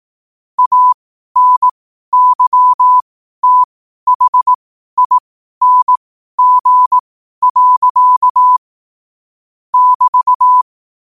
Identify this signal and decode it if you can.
Morse code, heard as a keyed tone, about 18 wpm, spaced as for 12 wpm, 1000 Hz. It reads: ANYTHING. <BT>
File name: qotdfiles / 12wpm